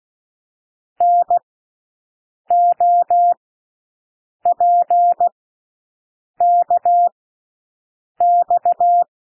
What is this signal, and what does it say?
Morse code, heard as a keyed tone, about 16 wpm, spaced as for 6 wpm, 700 Hz. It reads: NOPKX
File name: cwmp3 / L1 SOUND LETTERS 16 wpm